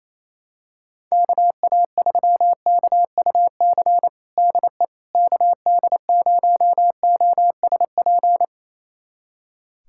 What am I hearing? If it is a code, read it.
Morse code, heard as a keyed tone, about 28 wpm, 700 Hz. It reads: KA3KUC DE KD0OSP
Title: USA Callsign DE Callsign 28wpm